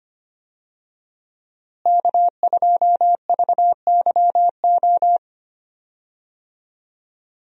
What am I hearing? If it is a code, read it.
Morse code, heard as a keyed tone, about 25 wpm, 700 Hz. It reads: K2VYO
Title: USA Call Signs - 5 Characters - Repeated 3x - 25wpm